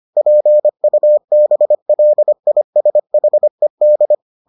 Morse code, heard as a keyed tone, about 25 wpm, 600 Hz. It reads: PUBLISHED